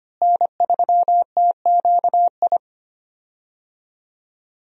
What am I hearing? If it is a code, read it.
Morse code, heard as a keyed tone, about 25 wpm, 700 Hz. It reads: N3TQI
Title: USA Call Signs - 5 Characters - Repeated 2x - 25wpm